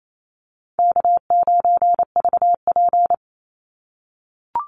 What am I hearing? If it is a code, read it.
Morse code, heard as a keyed tone, about 28 wpm, 700 Hz. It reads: K9VP
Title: Call Signs - 28wpm